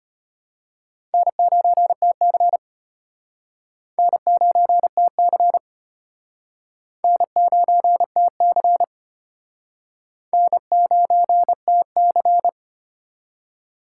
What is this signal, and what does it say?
Morse code, heard as a keyed tone, about 33 wpm, 700 Hz. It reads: N9TC N9TC N9TC N9TC